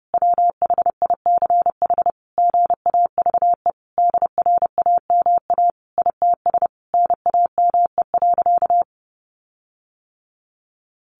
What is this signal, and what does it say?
Morse code, heard as a keyed tone, about 30 wpm, 700 Hz. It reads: WHICH GAVE DRAMA ITS NAME.
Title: Sentences from Top 1500 Words 30wpm